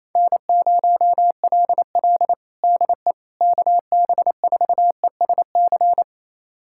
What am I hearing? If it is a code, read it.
Morse code, heard as a keyed tone, about 28 wpm, 700 Hz. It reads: N0LL DE KB4ESC
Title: USA Callsign DE Callsign 28wpm